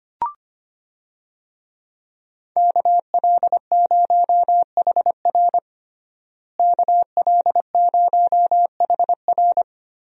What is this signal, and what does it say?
Morse code, heard as a keyed tone, about 25 wpm, 700 Hz. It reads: KL0HR KL0HR
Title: USA Call Signs - 5 Characters - Repeated 3x - 25wpm